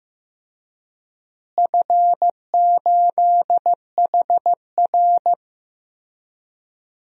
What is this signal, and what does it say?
Morse code, heard as a keyed tone, about 16 wpm, 700 Hz. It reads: F8HR
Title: DXCC-100 Call Signs - 4 Characters - Repeated 3x - 15wpm